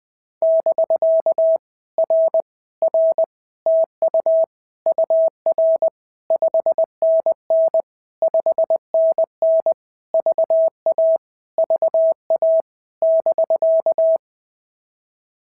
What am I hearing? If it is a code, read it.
Morse code, heard as a keyed tone, about 18 wpm, 650 Hz. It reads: <BK> R R TU UR 5NN 5NN VA VA <BK>
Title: POTA QSO Practice - 20wpm